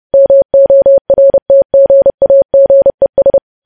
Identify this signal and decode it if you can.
Morse code, heard as a keyed tone, about 30 wpm, 550 Hz. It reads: MORTGAGES
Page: news headlines in morse code at 30 wpm